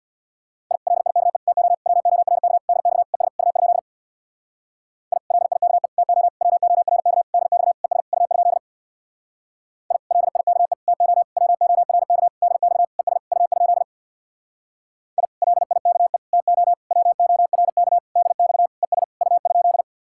Morse code, heard as a keyed tone, about 64 wpm, 700 Hz. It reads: I LIKE TO WORK DX ES U? I LIKE TO WORK DX ES U? I LIKE TO WORK DX ES U? I LIKE TO WORK DX ES U?